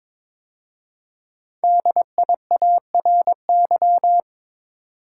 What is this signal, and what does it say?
Morse code, heard as a keyed tone, about 22 wpm, 700 Hz. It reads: DIARY
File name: 5 Letter Words - First 3 Letters Given - 22wpm